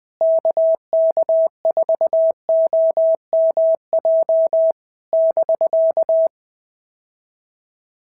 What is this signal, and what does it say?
Morse code, heard as a keyed tone, about 20 wpm, 650 Hz. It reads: KK4OMJ <BK>
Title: POTA QSO Practice - 20wpm